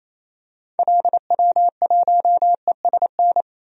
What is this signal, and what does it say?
Morse code, heard as a keyed tone, about 28 wpm, 700 Hz. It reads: LW1ESN
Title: DXCC-100 Call Signs - 6 Characters - 28wpm